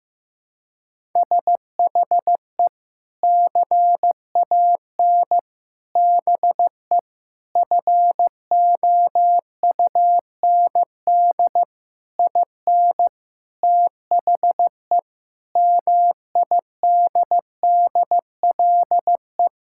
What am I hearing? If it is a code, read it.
Morse code, heard as a keyed tone, about 15 wpm, 700 Hz. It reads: SHE CAN BE FOUND IN THE MIDDLE
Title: Sentences from Top 1000 Words - Review Part 6 - 15wpm